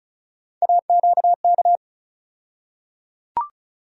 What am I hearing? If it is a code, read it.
Morse code, heard as a keyed tone, about 34 wpm, 700 Hz. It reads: AQK
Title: USA Call Sign Postfixes - 3 Letters 35wpm